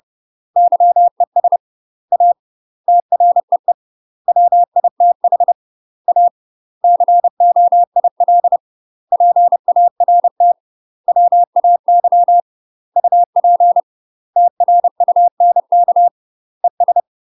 Morse code, heard as a keyed tone, about 30 wpm, 700 Hz. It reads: YES A TREE WITH A COIL PART WAY UP TRUNK ES
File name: UK MASTER QSOs TRAINING / 30WPM / UK MASTER QSOs 0000